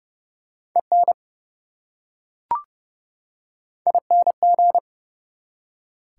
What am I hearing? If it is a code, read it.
Morse code, heard as a keyed tone, about 32 wpm, 700 Hz. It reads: EN ING
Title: Prefixes and Suffixes 30wpm